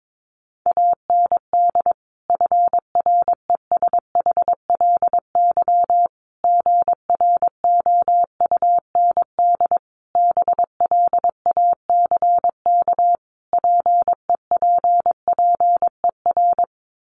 Morse code, heard as a keyed tone, about 22 wpm, 700 Hz. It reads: AND FRESHLY GROUND BLACK PEPPER
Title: Sets of 5 Words Encore 22wpm